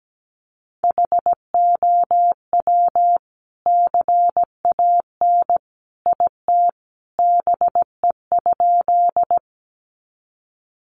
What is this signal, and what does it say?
Morse code, heard as a keyed tone, about 17 wpm, 700 Hz. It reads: HOW CAN IT BE?
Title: Sentences from Top 100 Words 17wpm